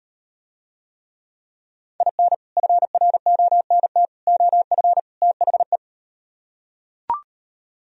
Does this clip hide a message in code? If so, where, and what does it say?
Yes: Morse code from 2.0 to 5.8 s, heard as a keyed tone, about 38 wpm, 700 Hz. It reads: IN FRONT OF THE